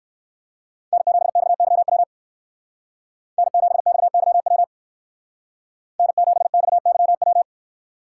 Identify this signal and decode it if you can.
Morse code, heard as a keyed tone, about 68 wpm, 700 Hz. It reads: N7XYW N7XYW N7XYW